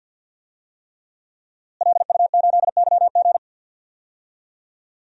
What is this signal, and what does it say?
Morse code, heard as a keyed tone, about 52 wpm, 700 Hz. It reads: LU8YG
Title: DXCC-100 Call Signs - 5 Characters - Repeated 3x - 50wpm